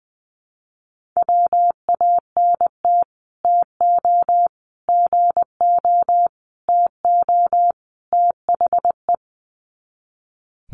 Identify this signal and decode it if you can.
Morse code, heard as a keyed tone, about 20 wpm, 700 Hz. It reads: WANT TO GO TO THE
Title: Sets of 5 Words Encore2 20wpm